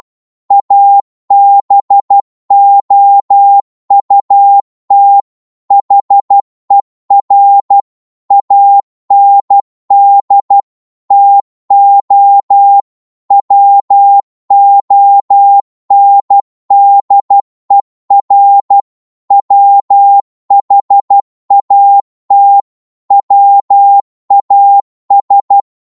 Morse code, heard as a keyed tone, about 12 wpm, 800 Hz. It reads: ABOUT HER AND TO WONDER WHAT WAS
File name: cwmp3 / psg-1-12wpm